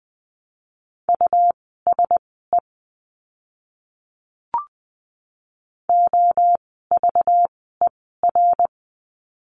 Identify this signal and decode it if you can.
Morse code, heard as a keyed tone, about 20 wpm, spaced as for 14 wpm, 700 Hz. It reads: USE OVER
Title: 230 Lesson 37f - Letter x - Words 20wpm